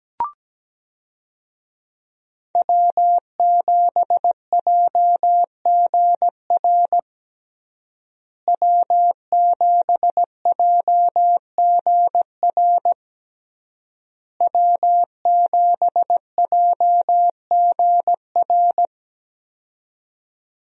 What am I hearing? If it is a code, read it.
Morse code, heard as a keyed tone, about 17 wpm, 700 Hz. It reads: W7JGR W7JGR W7JGR
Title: USA Call Signs - 5 Characters - Repeated 3x - 17wpm